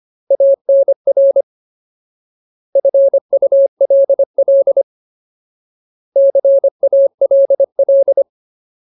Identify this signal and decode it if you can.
Morse code, heard as a keyed tone, about 25 wpm, 550 Hz. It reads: ANR FULL CALL
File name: Ultimate Word List / 25WPM Ultimate